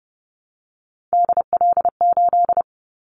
Morse code, heard as a keyed tone, about 30 wpm, 700 Hz. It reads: DL8
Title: Callsign Prefixes for Top 50 DXCC - 30wpm